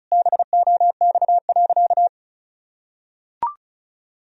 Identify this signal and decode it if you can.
Morse code, heard as a keyed tone, about 34 wpm, 700 Hz. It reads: BOX.
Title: Sentences from Top 1000 Words - Review Part 2 - 35wpm